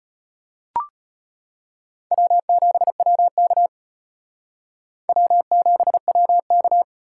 Morse code, heard as a keyed tone, about 32 wpm, 700 Hz. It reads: W7WK W7WK